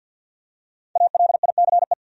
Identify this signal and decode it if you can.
Morse code, heard as a keyed tone, about 53 wpm, 700 Hz. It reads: ALICE
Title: USA Common Names - 50wpm